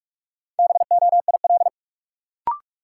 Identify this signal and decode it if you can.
Morse code, heard as a keyed tone, about 42 wpm, 700 Hz. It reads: BOIL
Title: Sets of 5 Words Encore - New Word Drill - 45wpm